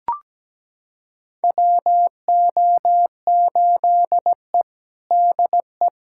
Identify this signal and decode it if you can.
Morse code, heard as a keyed tone, about 17 wpm, 700 Hz. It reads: WO8E DE
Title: USA Callsign DE Callsign - 4 Characters - 17wpm